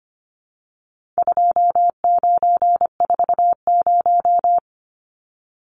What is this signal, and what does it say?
Morse code, heard as a keyed tone, about 25 wpm, 700 Hz. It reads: 2940
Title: Numbers - 4-Digit 25wpm